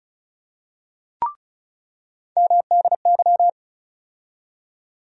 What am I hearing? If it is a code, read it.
Morse code, heard as a keyed tone, about 36 wpm, 700 Hz. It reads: MDY
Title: USA Call Sign Postfixes - 3 Letters 35wpm